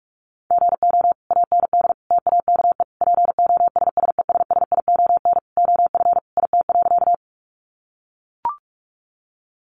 Morse code, heard as a keyed tone, about 44 wpm, 700 Hz. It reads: GO AND TAKE POSSESSION OF IT.
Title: Sentences from Top 1400 Words 45wpm